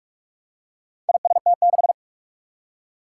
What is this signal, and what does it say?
Morse code, heard as a keyed tone, about 44 wpm, 700 Hz. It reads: IST6
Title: Four Challenging Letter-Number Combinations 45wpm